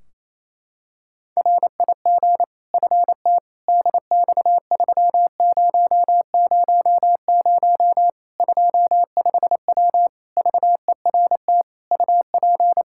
Morse code, heard as a keyed tone, about 27 wpm, 700 Hz. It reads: RIG FT DX3000 25W VERT UP
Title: Practice QSO - Exchange 2 - Info - 28wpm